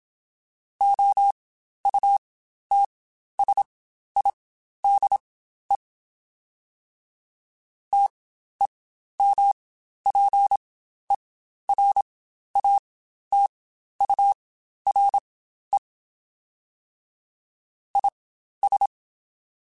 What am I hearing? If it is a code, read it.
Morse code, heard as a keyed tone, about 27 wpm, spaced as for 12 wpm, 800 Hz. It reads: OUTSIDE TEMPERATURE IS
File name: Spaced Characters / Spaced Characters-mp3s / Weather4 14 sp @25